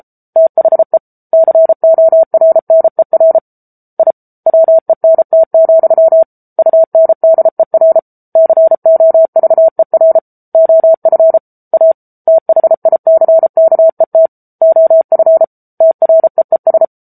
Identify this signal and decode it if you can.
Morse code, heard as a keyed tone, about 34 wpm, 650 Hz. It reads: THE CORNER I WENT, UNDER COVER OF A THICKET OF TREES
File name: Book Two / 07-Chapter 7 - 33 WPM